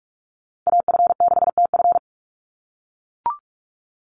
Morse code, heard as a keyed tone, about 44 wpm, 700 Hz. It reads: AF6TF
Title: USA Call Signs 45wpm